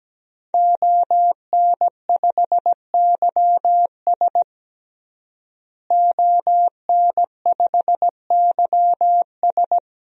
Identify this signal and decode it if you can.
Morse code, heard as a keyed tone, about 17 wpm, 700 Hz. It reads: ON5YS ON5YS
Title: DXCC-100 Call Signs - 5 Characters - Repeated 3x - 17wpm